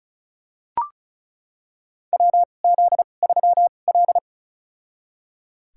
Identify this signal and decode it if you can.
Morse code, heard as a keyed tone, about 36 wpm, spaced as for 25 wpm, 700 Hz. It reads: WZ3L
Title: USA Call Signs - 4 Characters - 2x Character Spacing - 35wpm